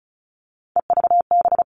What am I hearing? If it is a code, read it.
Morse code, heard as a keyed tone, about 34 wpm, 700 Hz. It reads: EVB
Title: All Cut Numbers - 3-Digit 35wpm